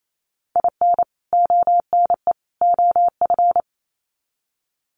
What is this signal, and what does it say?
Morse code, heard as a keyed tone, about 28 wpm, 700 Hz. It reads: IN ONE OF
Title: Sets of 3 Words 28wpm